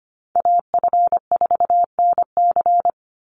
Morse code, heard as a keyed tone, about 25 wpm, 700 Hz. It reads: AF4NC
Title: Call Signs Repeated 3x - 25wpm